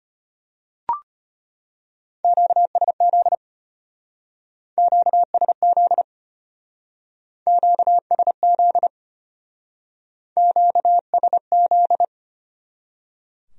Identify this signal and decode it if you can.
Morse code, heard as a keyed tone, about 31 wpm, 700 Hz. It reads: QSZ QSZ QSZ QSZ